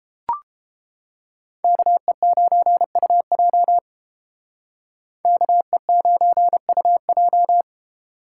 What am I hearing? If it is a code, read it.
Morse code, heard as a keyed tone, about 31 wpm, 700 Hz. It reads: KE9UJ KE9UJ